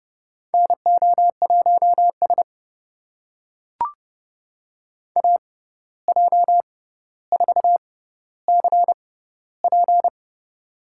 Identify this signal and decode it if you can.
Morse code, heard as a keyed tone, about 31 wpm, spaced as for 14 wpm, 700 Hz. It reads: NO1S AJ4CP
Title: USA Call Signs - 6x Character Spacing - 30wpm